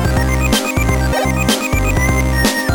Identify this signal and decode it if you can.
There is no code.